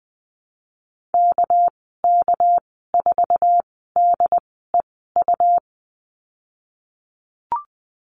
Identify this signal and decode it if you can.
Morse code, heard as a keyed tone, about 20 wpm, spaced as for 15 wpm, 700 Hz. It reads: KK4DEU
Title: 207 Lesson 33g - Letter j - Call Signs 20wpm